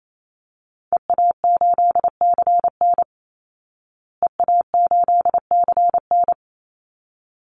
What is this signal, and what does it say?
Morse code, heard as a keyed tone, about 27 wpm, 700 Hz. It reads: EA8CN EA8CN
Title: Call Signs Repeated 3x - 28wpm